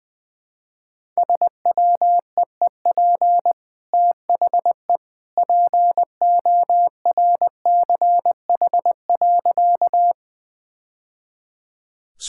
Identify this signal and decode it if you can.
Morse code, heard as a keyed tone, about 20 wpm, 700 Hz. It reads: SWEEP THE PORCH.
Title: Easy Sentences with Hints - 20wpm